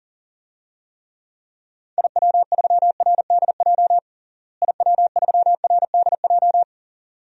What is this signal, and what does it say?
Morse code, heard as a keyed tone, about 40 wpm, 700 Hz. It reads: IW3RDJ IW3RDJ
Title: DXCC-100 Call Signs - 6 Characters - Repeated 2x - 40wpm